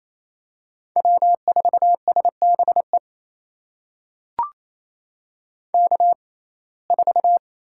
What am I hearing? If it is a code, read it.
Morse code, heard as a keyed tone, about 28 wpm, spaced as for 20 wpm, 700 Hz. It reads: W4SBE K4
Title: USA Call Signs - 6x Character Spacing - 28wpm